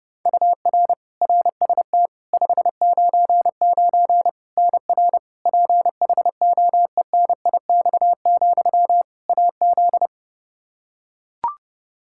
Morse code, heard as a keyed tone, about 30 wpm, 700 Hz. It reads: UR RST 599 NR PHOENIX, AZ
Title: Practice QSO - Exchange 1 - Beginning - 30wpm